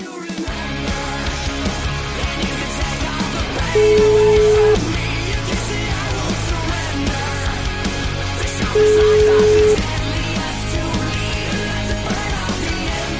Ringback tone (Europe, 425 Hz), with noise.